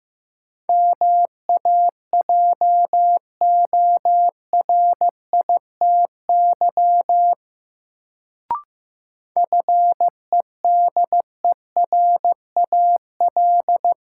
Morse code, heard as a keyed tone, about 15 wpm, 700 Hz. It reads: MAJORITY FEDERAL
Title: Sets of 3 Words - New Word Drill - 15wpm